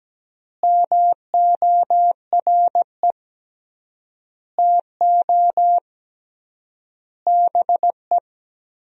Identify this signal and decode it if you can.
Morse code, heard as a keyed tone, about 17 wpm, 700 Hz. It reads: MORE TO BE
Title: 3x Word-Spacing - Sentences from Top 500 Words - 17wpm